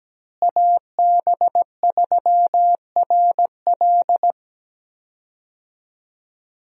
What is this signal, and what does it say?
Morse code, heard as a keyed tone, about 17 wpm, 700 Hz. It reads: AB3RL
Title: USA Call Signs - 5 Characters - Repeated 3x - 17wpm